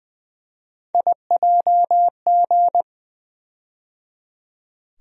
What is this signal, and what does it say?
Morse code, heard as a keyed tone, about 20 wpm, 700 Hz. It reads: IJG
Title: USA Call Sign Postfixes - 3 Letters 20wpm